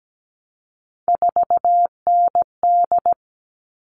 Morse code, heard as a keyed tone, about 17 wpm, 700 Hz. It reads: 4ND
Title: All Cut Numbers - 3-Digit 17wpm